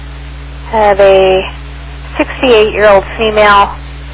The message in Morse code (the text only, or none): none